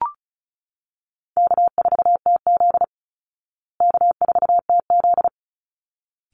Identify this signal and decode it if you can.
Morse code, heard as a keyed tone, about 34 wpm, 700 Hz. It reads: K4TZ K4TZ
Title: Easy Call Signs Repeated 2x - 35wpm